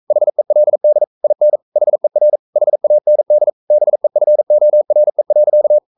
Morse code, heard as a keyed tone, about 44 wpm, 600 Hz. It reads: HELD IN HER HAND BEFORE.
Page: Romanen Never af Ken Follet på engelsk.